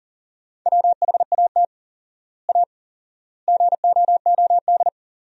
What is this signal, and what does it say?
Morse code, heard as a keyed tone, about 40 wpm, 700 Hz. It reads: WHAT A GOOD
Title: 4x Word-Spacing - Sentences from Top 200 Words - 40wpm